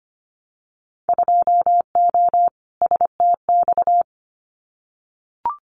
Morse code, heard as a keyed tone, about 25 wpm, 700 Hz. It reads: 2O STX
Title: Winter Field Day Exchanges 25wpm